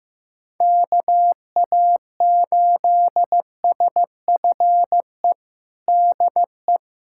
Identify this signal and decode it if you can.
Morse code, heard as a keyed tone, about 15 wpm, 700 Hz. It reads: KA8SFE DE
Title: USA Callsign DE Callsign - 6 Characters - 15wpm